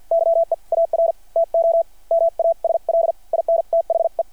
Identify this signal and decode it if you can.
Morse code, heard as a keyed tone, about 52 wpm, spaced as for 43 wpm, 660 Hz. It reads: YEAR TO MASL IN THE
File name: archived / CRSnet